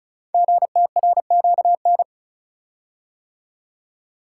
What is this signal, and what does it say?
Morse code, heard as a keyed tone, about 36 wpm, 700 Hz. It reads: GTRQN
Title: Code Groups - 5 Letters - 35wpm